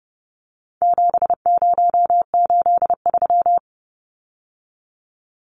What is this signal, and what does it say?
Morse code, heard as a keyed tone, about 30 wpm, 700 Hz. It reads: 7083
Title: Numbers - 4-Digit 30wpm